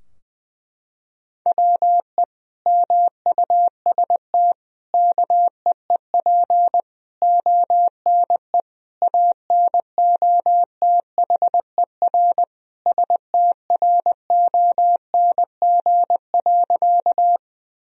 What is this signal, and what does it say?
Morse code, heard as a keyed tone, about 20 wpm, 700 Hz. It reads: WE MUST KEEP ONE ANOTHER STRONG.
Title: Sentences from Top 1000 Words - Review Part 3 - 20wpm